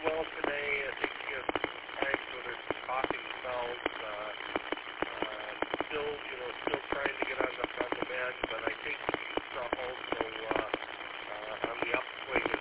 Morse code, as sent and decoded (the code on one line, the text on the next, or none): none